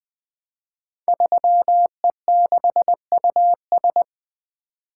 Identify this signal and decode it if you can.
Morse code, heard as a keyed tone, about 20 wpm, 700 Hz. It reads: 3E6US
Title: Code Groups - 5 Letter:Numbers - 20wpm